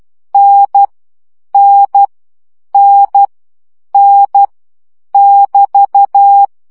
Morse code, heard as a keyed tone, about 12 wpm, 800 Hz. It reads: N N N N <BT>
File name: CW gyakorlas n 10 baud